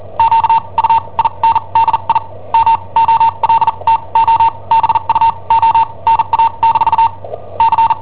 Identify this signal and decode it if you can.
Morse code, heard as a keyed tone, about 41 wpm, spaced as for 30 wpm, 910 Hz. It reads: QUINDI MOLTO BUONA- C
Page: Cassetta esercizi preparata da I7OHP per IW5ELP